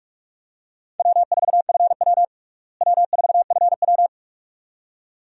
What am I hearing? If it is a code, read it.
Morse code, heard as a keyed tone, about 42 wpm, 700 Hz. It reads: W4FW W4FW
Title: USA Call Signs - 4 Characters - Repeated 2x - 45wpm